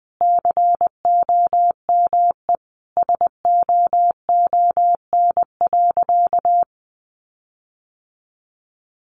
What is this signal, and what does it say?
Morse code, heard as a keyed tone, about 19 wpm, 700 Hz. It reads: COME SOON.
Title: Sentences from Top 1500 Words 20wpm